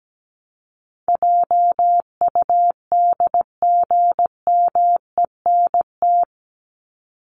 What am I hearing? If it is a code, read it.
Morse code, heard as a keyed tone, about 17 wpm, 700 Hz. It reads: JUDGMENT
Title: Top 1751-2000 Words 17wpm